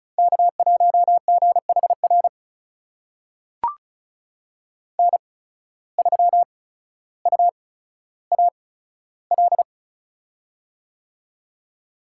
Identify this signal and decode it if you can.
Morse code, heard as a keyed tone, about 36 wpm, spaced as for 15 wpm, 700 Hz. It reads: K1GHR N3UAL
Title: USA Call Signs - 5 Characters - 8x Character Spacing - 35wpm